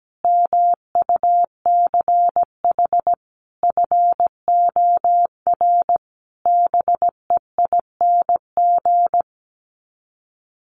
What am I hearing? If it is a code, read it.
Morse code, heard as a keyed tone, about 17 wpm, 700 Hz. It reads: MUCH FOR BEING
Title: Sets of 5 Words Encore 17wpm